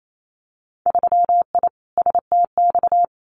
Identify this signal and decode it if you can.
Morse code, heard as a keyed tone, about 28 wpm, 700 Hz. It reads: 3I STX
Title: Winter Field Day Exchanges Encore 28wpm